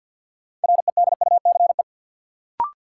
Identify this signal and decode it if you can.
Morse code, heard as a keyed tone, about 52 wpm, 700 Hz. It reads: REDUCE
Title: Sentences from Top 1100 Words - New Word Drill Alt - 50wpm